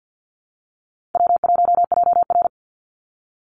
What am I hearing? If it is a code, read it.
Morse code, heard as a keyed tone, about 49 wpm, 700 Hz. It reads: W1JR
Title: Easy Call Signs Repeated 2x - 50wpm